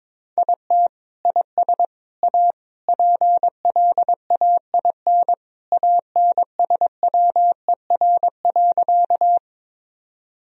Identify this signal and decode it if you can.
Morse code, heard as a keyed tone, about 22 wpm, 700 Hz. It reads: IT IS A PLAIN ANSWER.
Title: Sentences from Top 1000 Words - Review Part 2 - 22wpm